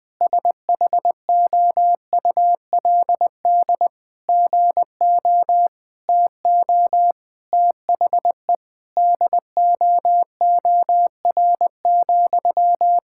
Morse code, heard as a keyed tone, about 20 wpm, 700 Hz. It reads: SHOULD GO TO THE DOOR,
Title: Sentences from Top 1600 Words 20wpm